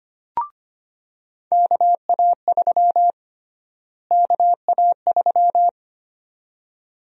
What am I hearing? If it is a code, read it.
Morse code, heard as a keyed tone, about 25 wpm, 700 Hz. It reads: KA3 KA3
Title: USA Call Sign Prefixes - Repeated 2x - 25wpm